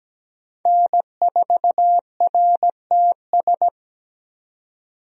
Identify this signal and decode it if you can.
Morse code, heard as a keyed tone, about 17 wpm, 700 Hz. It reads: N4RTS
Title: USA Call Signs - 5 Characters - 17wpm